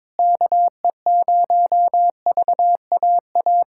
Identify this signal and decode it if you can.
Morse code, heard as a keyed tone, about 22 wpm, 700 Hz. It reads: KE0VAA
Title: USA Callsign DE Callsign - 6 Characters - 22wpm